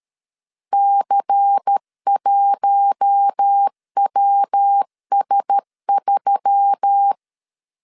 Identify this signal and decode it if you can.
Morse code, heard as a keyed tone, about 13 wpm, 800 Hz. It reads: C1WS3
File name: LEZIONI / 7 GF9Z / 7 GF9 12wpm 2319